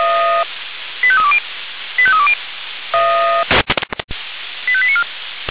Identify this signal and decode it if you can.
Five-tone selective call (ZVEI), heard as quick runs of five tones, tones of about 70 ms. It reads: 85320 85320 85894